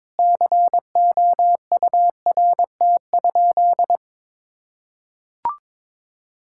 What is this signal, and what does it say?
Morse code, heard as a keyed tone, about 22 wpm, 700 Hz. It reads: COURT?
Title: Sentences from Top 1600 Words 22wpm